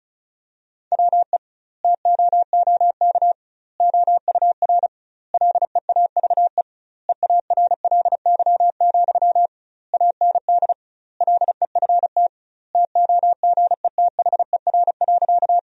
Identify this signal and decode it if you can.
Morse code, heard as a keyed tone, about 36 wpm, 700 Hz. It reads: WE TOOK OUR LEAVE EARLY, AND LEFT TOGETHER.
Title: 2x Word-Spacing - Sentences from Top 400 Words - 35wpm